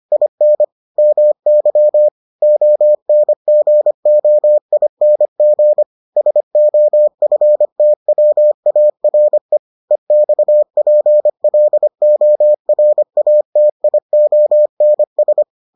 Morse code, heard as a keyed tone, about 25 wpm, 600 Hz. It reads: IN MY ONGOING SOFTWARE EXPLORATIONS